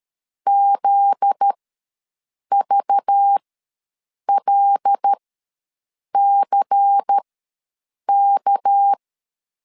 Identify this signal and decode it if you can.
Morse code, heard as a keyed tone, about 13 wpm, spaced as for 7 wpm, 800 Hz. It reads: ZVLCK
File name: LEZIONI / 8 YJ45 / 8 YJ ws 12wpm 3342